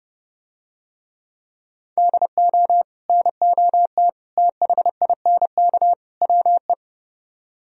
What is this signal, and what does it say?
Morse code, heard as a keyed tone, about 29 wpm, 700 Hz. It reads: DO NOT THINK WE